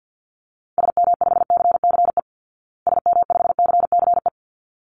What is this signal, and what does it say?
Morse code, heard as a keyed tone, about 50 wpm, 700 Hz. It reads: SM5CCE SM5CCE
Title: Call Signs Repeated 3x - 50wpm